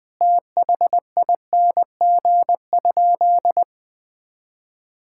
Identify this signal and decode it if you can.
Morse code, heard as a keyed tone, about 20 wpm, 700 Hz. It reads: THING?
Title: Sentences with 7-Word Phrases - Spoken Beginning - 20wpm